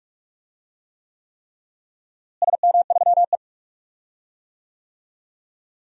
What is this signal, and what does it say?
Morse code, heard as a keyed tone, about 46 wpm, 700 Hz. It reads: SM3E
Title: DXCC-100 Call Signs - 4 Characters - Repeated 3x - 45wpm